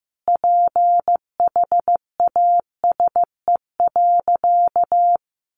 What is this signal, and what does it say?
Morse code, heard as a keyed tone, about 15 wpm, 700 Hz. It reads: PHASE.
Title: Sentences from Top 1200 Words 15wpm